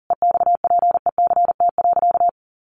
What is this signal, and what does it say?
Morse code, heard as a keyed tone, about 40 wpm, 700 Hz. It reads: EXPECT.
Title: Sentences from Top 1200 Words 40wpm